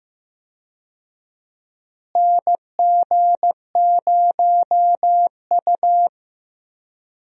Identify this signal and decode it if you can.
Morse code, heard as a keyed tone, about 15 wpm, 700 Hz. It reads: NG0U
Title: USA Call Signs - 4 Characters - Repeated 3x - 15wpm